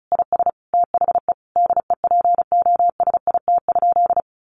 Morse code, heard as a keyed tone, about 34 wpm, 700 Hz. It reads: IS THE DEPOSIT?
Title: Sentences from Top 1300 Words 35wpm